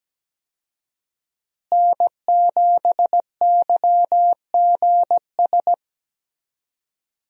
Morse code, heard as a keyed tone, about 17 wpm, 700 Hz. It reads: N7YGS
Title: USA Call Signs - 5 Characters - Repeated 2x - 17wpm